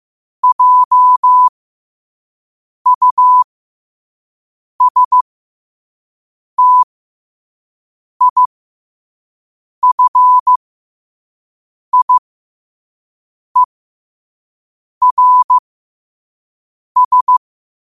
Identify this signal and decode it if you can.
Morse code, heard as a keyed tone, about 15 wpm, spaced as for 5 wpm, 1000 Hz. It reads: JUSTIFIERS
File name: qotdfiles / 5wpm